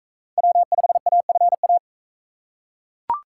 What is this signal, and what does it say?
Morse code, heard as a keyed tone, about 40 wpm, 700 Hz. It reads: W5AFA